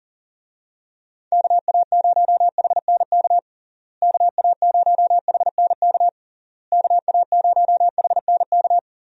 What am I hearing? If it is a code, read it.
Morse code, heard as a keyed tone, about 40 wpm, 700 Hz. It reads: KA0HNK KA0HNK KA0HNK
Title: USA Call Signs - 6 Characters - Repeated 3x - 40wpm